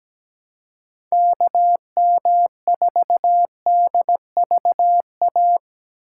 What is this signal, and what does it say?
Morse code, heard as a keyed tone, about 17 wpm, 700 Hz. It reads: KM4DVA
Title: USA Call Signs - 6 Characters - 3x Character Spacing - 17wpm